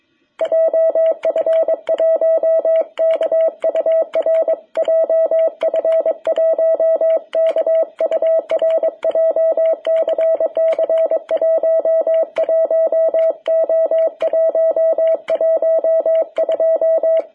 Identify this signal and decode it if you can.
Morse code, heard as a keyed tone, about 22 wpm, 660 Hz. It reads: JF1KUR JF1KUR JCC 11O112
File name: jh8cbh / 2021cw